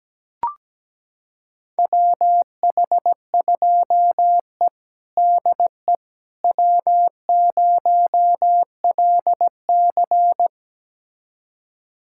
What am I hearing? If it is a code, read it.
Morse code, heard as a keyed tone, about 17 wpm, 700 Hz. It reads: WH2E DE W0LC
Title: USA Callsign DE Callsign - 4 Characters - 17wpm